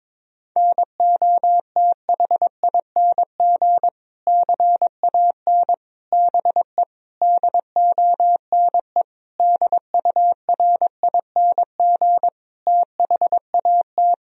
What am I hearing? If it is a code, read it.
Morse code, heard as a keyed tone, about 22 wpm, 700 Hz. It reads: NOTHING CAN BE DONE DURING THAT
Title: Sentences from Top 1000 Words - Review Part 2 - 22wpm